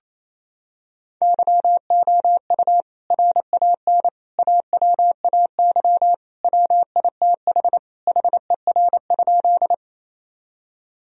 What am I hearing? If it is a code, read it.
Morse code, heard as a keyed tone, about 28 wpm, 700 Hz. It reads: YOU RAN AWAY WITH HER?
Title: Sentences from Top 1000 Words - Review Part 3 - 28wpm